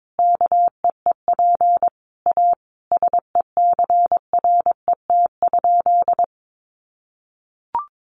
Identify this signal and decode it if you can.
Morse code, heard as a keyed tone, about 22 wpm, 700 Hz. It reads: KEEP A SECRET?
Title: Sentences from Top 1200 Words 22wpm